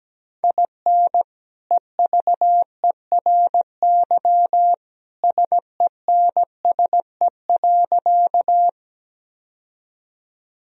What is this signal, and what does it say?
Morse code, heard as a keyed tone, about 17 wpm, 700 Hz. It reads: IN EVERY SENSE.
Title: Sentences from Top 1000 Words - Review Part 4 - 17wpm